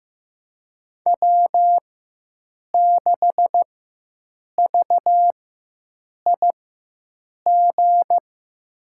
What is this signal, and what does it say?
Morse code, heard as a keyed tone, about 15 wpm, spaced as for 7 wpm, 700 Hz. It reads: W6VIG
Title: USA Call Signs - 5 Characters - 4x Character Spacing - 15wpm